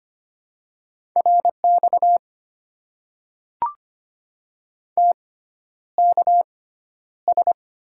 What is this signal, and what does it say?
Morse code, heard as a keyed tone, about 25 wpm, spaced as for 11 wpm, 700 Hz. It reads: RX TKS
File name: Top 90 QSO Elements - 6x Character Spacing - 25wpm